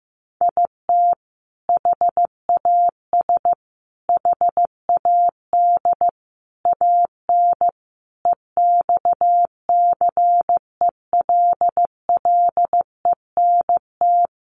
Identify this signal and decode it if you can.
Morse code, heard as a keyed tone, about 15 wpm, 700 Hz. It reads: IT HAS HAD AN EXCELLENT
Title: Sentences from Top 1100 Words 15wpm